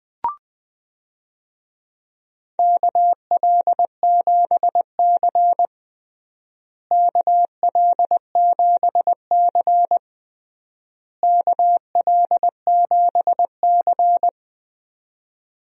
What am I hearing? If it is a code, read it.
Morse code, heard as a keyed tone, about 20 wpm, 700 Hz. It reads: KL7C KL7C KL7C
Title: DXCC-100 Call Signs - 4 Characters - Repeated 3x - 20wpm